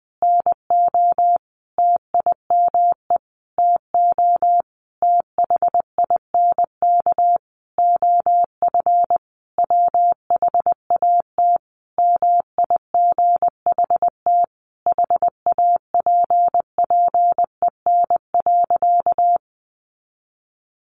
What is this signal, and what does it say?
Morse code, heard as a keyed tone, about 20 wpm, 700 Hz. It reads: NO TIME TO THINK OF WHAT MIGHT HAPPEN.
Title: Sentences from Top 400 Words 20wpm